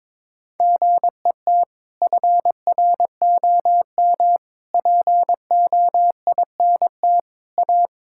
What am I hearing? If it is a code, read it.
Morse code, heard as a keyed tone, about 22 wpm, 700 Hz. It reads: GET FROM POINT A
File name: practice / 22wpm / Sets of 7 Words - 22wpm